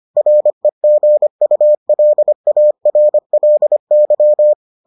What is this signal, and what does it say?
Morse code, heard as a keyed tone, about 25 wpm, 600 Hz. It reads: REGULARLY